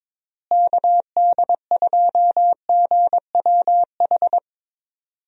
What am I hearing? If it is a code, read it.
Morse code, heard as a keyed tone, about 21 wpm, 700 Hz. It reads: KD2GWH
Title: USA Call Signs - 6 Characters - Repeated 3x - 22wpm